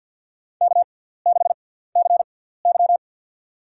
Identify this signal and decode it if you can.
Morse code, heard as a keyed tone, about 53 wpm, spaced as for 17 wpm, 700 Hz. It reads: K6CY
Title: USA Call Signs - 4 Characters - 6x Character Spacing - 50wpm